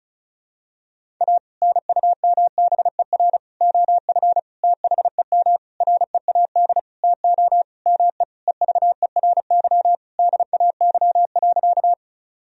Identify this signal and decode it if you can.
Morse code, heard as a keyed tone, about 36 wpm, 700 Hz. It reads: A NUMBER OF THEM READ TO ME EVERY DAY.
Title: Sentences from Top 1000 Words - Review Part 1 - 35wpm